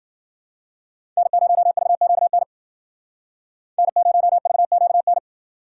Morse code, heard as a keyed tone, about 60 wpm, 700 Hz. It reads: N0VQN N0VQN